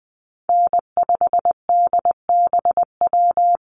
Morse code, heard as a keyed tone, about 20 wpm, 700 Hz. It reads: N5DBW
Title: USA Call Signs Repeated 2x 20wpm